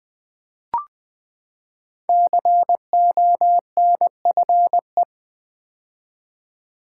Morse code, heard as a keyed tone, about 20 wpm, 700 Hz. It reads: CONFE
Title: Build A Word - 5 Letters At A Time - 10-Letter Words - 20wpm